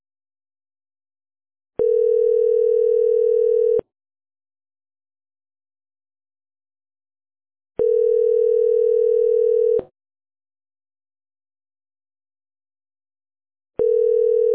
Ringback tone (North America), 440 + 480 Hz.